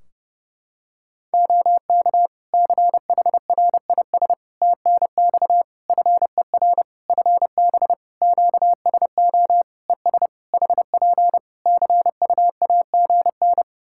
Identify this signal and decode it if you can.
Morse code, heard as a keyed tone, about 30 wpm, 700 Hz. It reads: OK CHRIS TNX FER FB QSO ES HP CUAGN
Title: Practice QSO - Exchange 4 - Ending - 30wpm